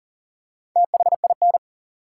Morse code, heard as a keyed tone, about 40 wpm, 700 Hz. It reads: THIN
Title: Four Letter Words - 40wpm